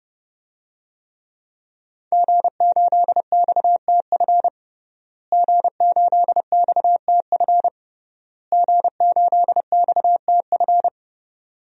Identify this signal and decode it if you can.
Morse code, heard as a keyed tone, about 30 wpm, 700 Hz. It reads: G8XTF G8XTF G8XTF
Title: DXCC-100 Call Signs - 5 Characters - Repeated 3x - 30wpm